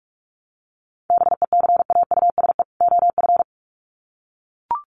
Morse code, heard as a keyed tone, about 44 wpm, 700 Hz. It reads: BECAUSE OF
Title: Sets of 2 Words 45wpm